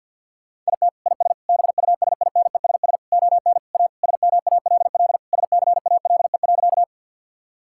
Morse code, heard as a keyed tone, about 50 wpm, 700 Hz. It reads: IT IS BUSINESS ON A SMALL SCALE.
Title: Sentences from Top 1000 Words - Review Part 5 - 50wpm